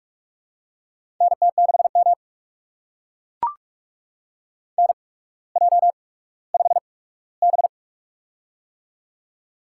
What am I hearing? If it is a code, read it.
Morse code, heard as a keyed tone, about 44 wpm, spaced as for 19 wpm, 700 Hz. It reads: NT6M NJ5B
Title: USA Call Signs - 4 Characters - 8x Character Spacing - 45wpm